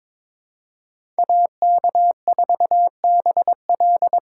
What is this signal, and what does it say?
Morse code, heard as a keyed tone, about 22 wpm, 700 Hz. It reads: AK4BL
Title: USA Call Signs - 5 Characters - 8x Character Spacing - 22wpm